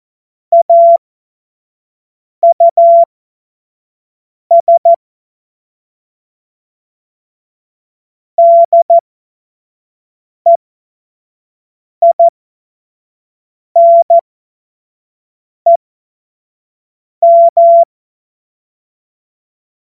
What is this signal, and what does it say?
Morse code, heard as a keyed tone, about 14 wpm, spaced as for 5 wpm, 680 Hz. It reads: AUS DEINEM
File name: Morsekurs / Witze / Witz 08. Februar 2022 25 F70